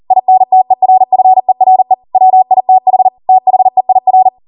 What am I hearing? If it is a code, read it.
Morse code, heard as a keyed tone, about 40 wpm, 750 Hz. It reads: INTERFERE WITH THEIR
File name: Morse / 40WPM